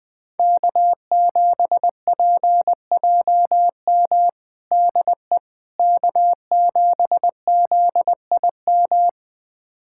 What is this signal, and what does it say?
Morse code, heard as a keyed tone, about 20 wpm, 700 Hz. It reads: K7PJM DE K7ZIM
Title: USA Callsign DE Callsign - 5 Characters - 20wpm